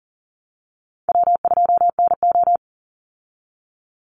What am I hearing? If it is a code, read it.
Morse code, heard as a keyed tone, about 39 wpm, 700 Hz. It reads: W2NO
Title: Easy Call Signs Repeated 2x - 40wpm